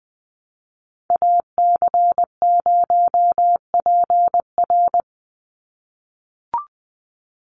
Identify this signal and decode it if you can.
Morse code, heard as a keyed tone, about 20 wpm, 700 Hz. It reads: AC0PR
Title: NA SOTA Chaser Callsigns 20wpm